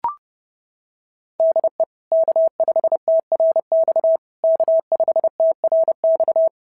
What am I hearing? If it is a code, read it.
Morse code, heard as a keyed tone, about 30 wpm, 660 Hz. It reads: DE K5TRX K5TRX
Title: SOTA QSO Practice 30wpm